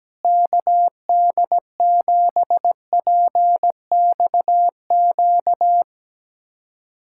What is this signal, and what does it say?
Morse code, heard as a keyed tone, about 17 wpm, 700 Hz. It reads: KD7PXQ
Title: USA Callsign DE Callsign - 6 Characters - 17wpm